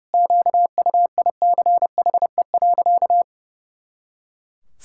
Morse code, heard as a keyed tone, about 32 wpm, 700 Hz. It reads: QUICHE.
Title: Pangram Challenge v2 - 30wpm